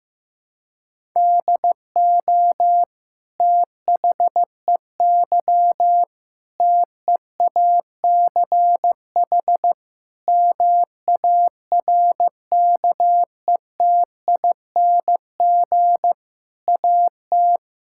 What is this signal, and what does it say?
Morse code, heard as a keyed tone, about 15 wpm, 700 Hz. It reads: DO THEY TEACH MARKETING AT
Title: Sentences from Top 1000 Words - Review Part 5 - 15wpm